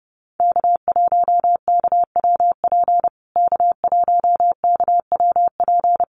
Morse code, heard as a keyed tone, about 30 wpm, 700 Hz. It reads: K1KWP K1KWP
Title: USA Call Signs Repeated 3x 30wpm